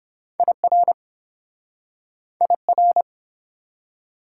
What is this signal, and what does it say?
Morse code, heard as a keyed tone, about 30 wpm, 700 Hz. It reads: IR IR